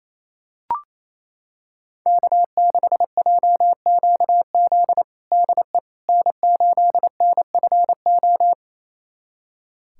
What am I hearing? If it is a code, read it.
Morse code, heard as a keyed tone, about 27 wpm, 700 Hz. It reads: K6JQZ DE N8NFO
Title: USA Callsign DE Callsign - 5 Characters - 28wpm